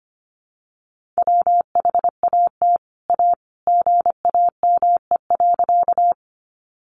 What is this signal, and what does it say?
Morse code, heard as a keyed tone, about 25 wpm, 700 Hz. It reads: WHAT A GAME.
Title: Sentences from Top 500 Words 25wpm